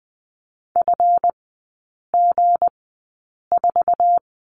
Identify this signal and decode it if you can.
Morse code, heard as a keyed tone, about 20 wpm, spaced as for 8 wpm, 690 Hz. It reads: FG4
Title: 198 Lesson 32e - Letter k - Three Characters 20wpm